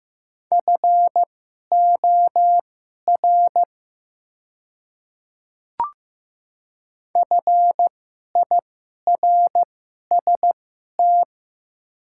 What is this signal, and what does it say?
Morse code, heard as a keyed tone, about 15 wpm, spaced as for 11 wpm, 700 Hz. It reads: FOR FIRST
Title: 2x Character Spacing - Top 100 Words - 15wpm